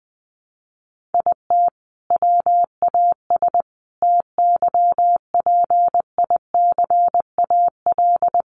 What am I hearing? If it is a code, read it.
Morse code, heard as a keyed tone, about 20 wpm, 700 Hz. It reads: IT WAS TYPICAL
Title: Sentences from Top 1100 Words 20wpm